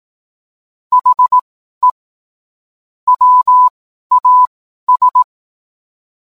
Morse code, heard as a keyed tone, about 18 wpm, spaced as for 12 wpm, 1000 Hz. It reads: HE WAS